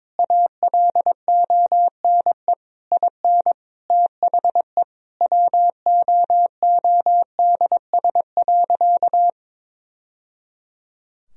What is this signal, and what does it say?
Morse code, heard as a keyed tone, about 22 wpm, 700 Hz. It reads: ALONE IN THE WOODS.
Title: Sentences from Top 1000 Words - Review Part 2 - 22wpm